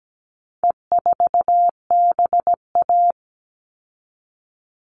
Morse code, heard as a keyed tone, about 17 wpm, 700 Hz. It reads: E4BA
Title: All Cut Numbers - 4-Digit 17wpm